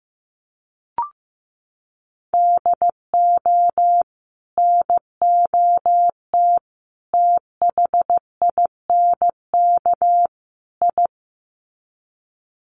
Morse code, heard as a keyed tone, about 15 wpm, 700 Hz. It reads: DO NOT THINK I
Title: Sets of 4 Words 15wpm